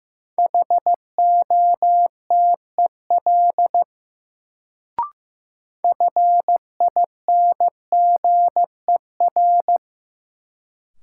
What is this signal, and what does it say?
Morse code, heard as a keyed tone, about 15 wpm, 700 Hz. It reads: HOTEL FINGER
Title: Sentences from Top 1600 Words - New Word Drill - 15wpm